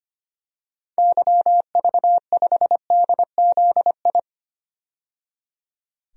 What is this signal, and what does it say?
Morse code, heard as a keyed tone, about 25 wpm, 700 Hz. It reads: YV5DZI
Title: DXCC-100 Call Signs - 6 Characters - 25wpm